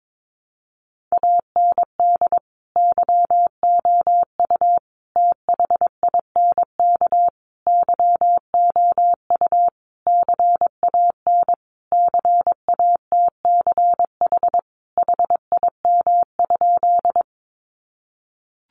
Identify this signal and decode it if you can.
Morse code, heard as a keyed tone, about 22 wpm, 700 Hz. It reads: AND YOU THINK YOU CAN CATCH HIM?
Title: Sentences from Top 1500 Words 22wpm